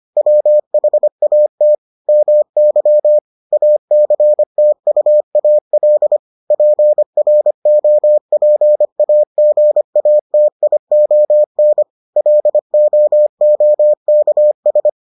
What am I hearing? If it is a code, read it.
Morse code, heard as a keyed tone, about 25 wpm, 600 Hz. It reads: WHAT MY ACTUAL PROPAGATION LOOKS